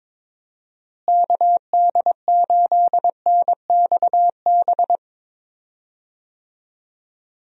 Morse code, heard as a keyed tone, about 21 wpm, 700 Hz. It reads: KD8NXB
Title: USA Call Signs - 6 Characters - Repeated 3x - 22wpm